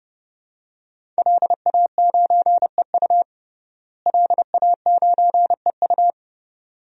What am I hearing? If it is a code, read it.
Morse code, heard as a keyed tone, about 30 wpm, 700 Hz. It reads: LA9EU LA9EU
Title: DXCC-100 Call Signs - 5 Characters - Repeated 2x - 30wpm